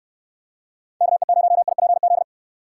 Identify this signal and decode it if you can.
Morse code, heard as a keyed tone, about 69 wpm, 700 Hz. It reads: KE0IPZ